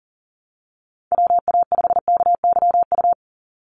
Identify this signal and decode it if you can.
Morse code, heard as a keyed tone, about 41 wpm, 700 Hz. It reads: WA5KYU
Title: USA Call Signs 40wpm